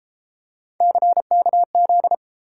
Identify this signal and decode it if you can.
Morse code, heard as a keyed tone, about 28 wpm, 700 Hz. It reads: CKZ